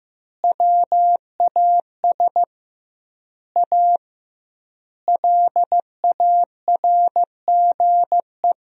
Morse code, heard as a keyed tone, about 15 wpm, 700 Hz. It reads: WAS A LARGE
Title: 2x Word-Spacing - Sentences from Top 500 Words - 15wpm